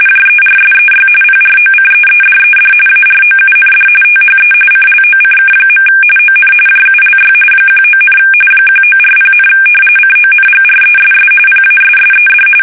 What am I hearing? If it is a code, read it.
RTTY, about 45.45 baud. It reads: JERSEY PROCEEDED AROUND THE CLOCK. ON JULY 19, 1940, CONGRESS PASSED
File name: fsk850 / battleshipnj